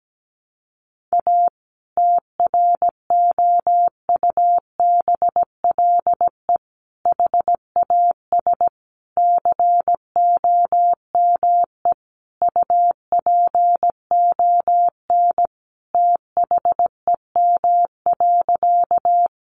Morse code, heard as a keyed tone, about 17 wpm, 700 Hz. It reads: A TROUBLE HAS COME UPON THEM.
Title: Sentences from Top 1200 Words 17wpm